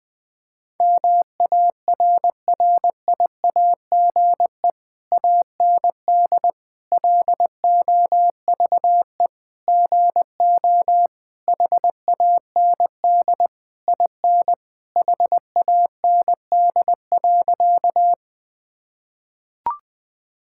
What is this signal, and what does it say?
Morse code, heard as a keyed tone, about 20 wpm, 700 Hz. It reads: MARRIAGE AND LOVE GO HAND IN HAND.
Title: Sentences from Top 1000 Words - Review Part 5 - 20wpm